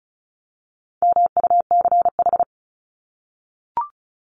Morse code, heard as a keyed tone, about 34 wpm, 700 Hz. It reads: MUCH
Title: Top 200 Words 35wpm